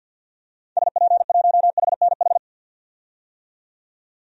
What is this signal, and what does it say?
Morse code, heard as a keyed tone, about 51 wpm, 700 Hz. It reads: SP1HNH
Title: DXCC-100 Call Signs - 6 Characters - 50wpm